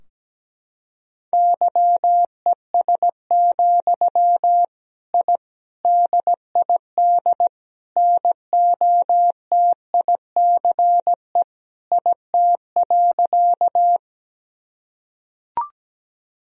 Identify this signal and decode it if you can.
Morse code, heard as a keyed tone, about 17 wpm, 700 Hz. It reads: YES, I DID NOTICE IT.
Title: Sentences from Top 1000 Words - Review Part 2 - 17wpm